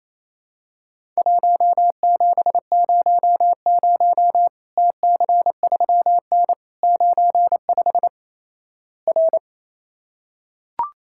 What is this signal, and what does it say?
Morse code, heard as a keyed tone, about 28 wpm, 700 Hz. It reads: 1700 TC3N 95 R
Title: Worked All Europe DX Contest - QTC Exchange - 28wpm